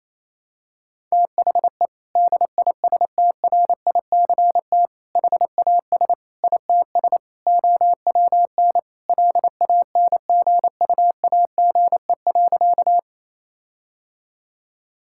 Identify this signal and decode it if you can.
Morse code, heard as a keyed tone, about 27 wpm, 700 Hz. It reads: THE DISTRICT HAS ITS OWN LANGUAGE.
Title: Sentences from Top 1600 Words 28wpm